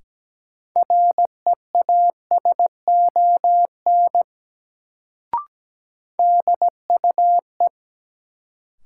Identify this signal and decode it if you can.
Morse code, heard as a keyed tone, about 17 wpm, 700 Hz. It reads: REASON DUE
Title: Sets of 5 Words Encore - New Word Drill - 17wpm